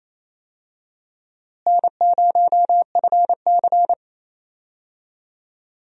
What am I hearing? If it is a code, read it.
Morse code, heard as a keyed tone, about 28 wpm, 700 Hz. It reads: N0FC
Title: USA Call Signs - 4 Characters - Repeated 2x - 28wpm